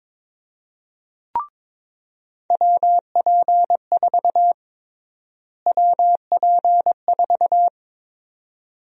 Morse code, heard as a keyed tone, about 22 wpm, 700 Hz. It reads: WP4 WP4
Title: USA Call Sign Prefixes - Repeated 2x - 22wpm